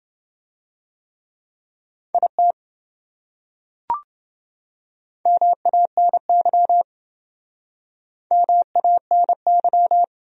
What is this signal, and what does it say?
Morse code, heard as a keyed tone, about 27 wpm, 700 Hz. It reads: IT MANY MANY